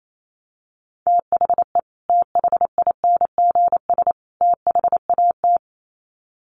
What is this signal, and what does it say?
Morse code, heard as a keyed tone, about 28 wpm, 700 Hz. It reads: THE THINGS THAT
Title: Sets of 3 Words 28wpm